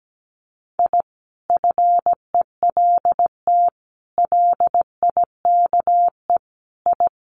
Morse code, heard as a keyed tone, about 17 wpm, 700 Hz. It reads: I FELT LIKE I
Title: Sets of 5 Words Encore 17wpm